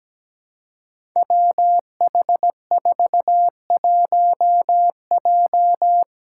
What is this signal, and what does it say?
Morse code, heard as a keyed tone, about 17 wpm, 700 Hz. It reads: WH41J
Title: Five Challenging Letter-Number Combinations 17wpm